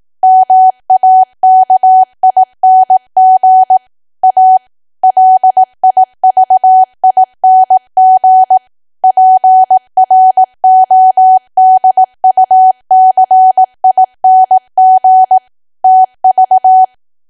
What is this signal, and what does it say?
Morse code, heard as a keyed tone, about 18 wpm, 750 Hz. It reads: MAKING A LIVING PRODUCING TV
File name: Archive / 18 WPM / 18WPM